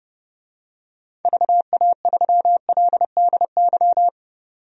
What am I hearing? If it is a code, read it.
Morse code, heard as a keyed tone, about 30 wpm, 700 Hz. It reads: VA3LDY
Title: DXCC-100 Call Signs - 6 Characters - 30wpm